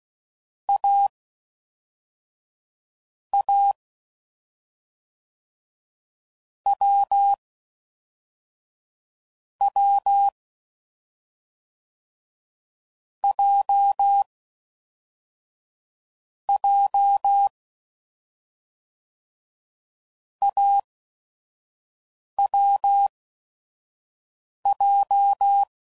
Morse code, heard as a keyed tone, about 16 wpm, spaced as for 5 wpm, 800 Hz. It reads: A A W W J J AWJ
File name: Combinations / Difficult Combos / awj1 ndb6 gz7